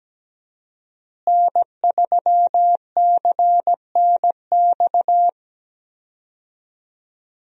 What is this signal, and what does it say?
Morse code, heard as a keyed tone, about 17 wpm, 700 Hz. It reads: N3CNX
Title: USA Call Signs - 5 Characters - Repeated 2x - 17wpm